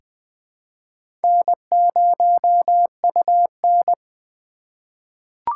Morse code, heard as a keyed tone, about 20 wpm, 700 Hz. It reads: N0UN
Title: USA Call Signs - 4 Characters - 2x Character Spacing - 20wpm